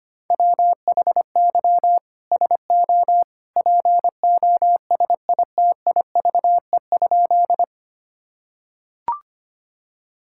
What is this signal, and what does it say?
Morse code, heard as a keyed tone, about 25 wpm, 700 Hz. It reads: WHY SO POSITIVE?
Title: Sentences from Top 1000 Words - Review Part 5 - 25wpm